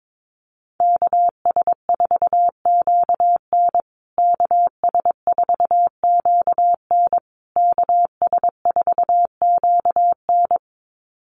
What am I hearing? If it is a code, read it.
Morse code, heard as a keyed tone, about 22 wpm, 700 Hz. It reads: KS4QN KS4QN KS4QN
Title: USA Call Signs Repeated 3x 22wpm